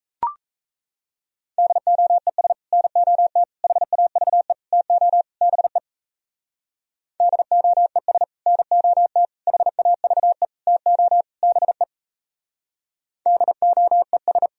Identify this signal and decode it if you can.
Morse code, heard as a keyed tone, about 38 wpm, 700 Hz. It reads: DOES NOT HAVE TO BE DOES NOT HAVE TO BE DOES